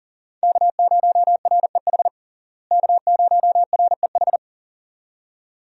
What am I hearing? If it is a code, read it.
Morse code, heard as a keyed tone, about 37 wpm, 700 Hz. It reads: K0REH K0REH
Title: USA Call Signs - 5 Characters - Repeated 2x - 40wpm